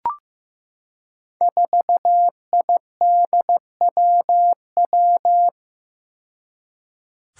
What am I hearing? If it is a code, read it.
Morse code, heard as a keyed tone, about 16 wpm, 700 Hz. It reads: 4IDWW
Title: Five Challenging Letter-Number Combinations 15wpm